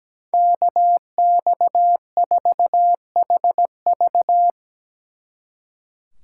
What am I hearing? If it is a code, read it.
Morse code, heard as a keyed tone, about 18 wpm, 700 Hz. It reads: KX4HV
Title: Speed Racing - USA 2x2 Call Signs 17wpm